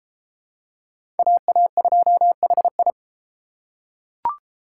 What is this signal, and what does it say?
Morse code, heard as a keyed tone, about 32 wpm, 700 Hz. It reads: AA2HI